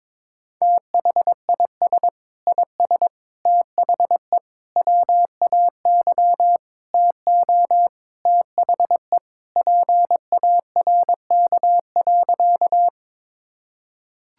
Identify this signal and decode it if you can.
Morse code, heard as a keyed tone, about 22 wpm, 700 Hz. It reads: THIS IS THE WAY TO THE PARK.
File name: practice / 22wpm / Sentences with 4-Word Phrases - 22wpm